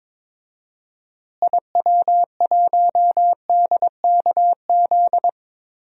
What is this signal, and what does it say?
Morse code, heard as a keyed tone, about 22 wpm, 700 Hz. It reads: IW1DKZ
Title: DXCC-100 Call Signs - 6 Characters - Repeated 3x - 22wpm